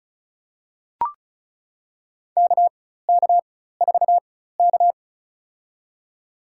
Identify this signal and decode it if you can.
Morse code, heard as a keyed tone, about 34 wpm, spaced as for 16 wpm, 700 Hz. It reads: KK4K
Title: USA Call Signs - 4x Character Spacing - 35wpm